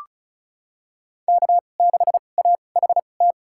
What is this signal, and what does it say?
Morse code, heard as a keyed tone, about 36 wpm, spaced as for 25 wpm, 700 Hz. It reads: K6AHT
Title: USA Call Signs - 5 Characters - 2x Character Spacing - 35wpm